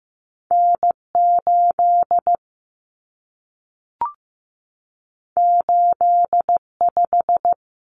Morse code, heard as a keyed tone, about 15 wpm, 700 Hz. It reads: N8 85
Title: Common Cut Numbers - 2-Digit 15wpm